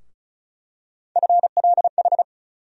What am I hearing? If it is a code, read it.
Morse code, heard as a keyed tone, about 34 wpm, 700 Hz. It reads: FLH
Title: USA Call Sign Postfixes - 3 Letters 35wpm